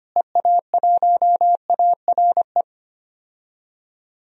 Morse code, heard as a keyed tone, about 25 wpm, 700 Hz. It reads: EA1ARE
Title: DXCC-100 Call Signs - 6 Characters - Repeated 2x - 25wpm